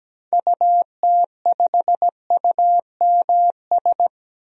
Morse code, heard as a keyed tone, about 17 wpm, 700 Hz. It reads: UT5UMS
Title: DXCC-100 Call Signs - 6 Characters - 17wpm